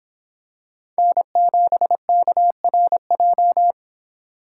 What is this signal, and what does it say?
Morse code, heard as a keyed tone, about 26 wpm, 700 Hz. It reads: N7KRJ